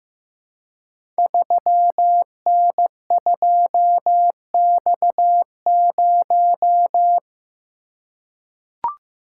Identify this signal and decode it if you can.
Morse code, heard as a keyed tone, about 15 wpm, 700 Hz. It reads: 3N2X0
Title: Code Groups - 5 Letter:Numbers - 15wpm